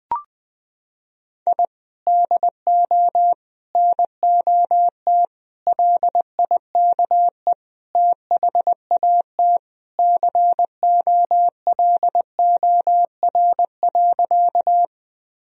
Morse code, heard as a keyed tone, about 21 wpm, 700 Hz. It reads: I DO NOT LIKE THAT COLOR.
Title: Sentences from Top 1000 Words - Review Part 2 - 20wpm